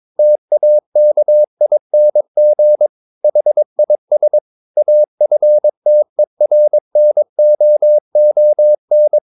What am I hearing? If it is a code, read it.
Morse code, heard as a keyed tone, about 22 wpm, 600 Hz. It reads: TAKING HIS AFTERNOON
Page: CW med Gnister